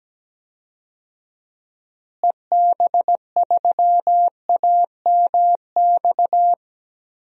Morse code, heard as a keyed tone, about 17 wpm, 700 Hz. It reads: EB3AMX
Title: DXCC-100 Call Signs - 6 Characters - Repeated 3x - 17wpm